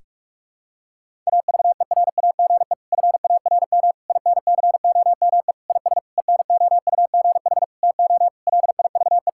Morse code, heard as a keyed tone, about 41 wpm, 700 Hz. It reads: AVERAGE FARM INCOME IS ENOUGH TO LIVE
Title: Sentences from Top 1000 Words - Review Part 6 - 45wpm